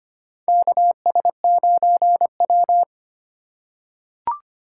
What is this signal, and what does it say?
Morse code, heard as a keyed tone, about 25 wpm, 700 Hz. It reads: KS9W
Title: K1USN SST QSO Practice - 25wpm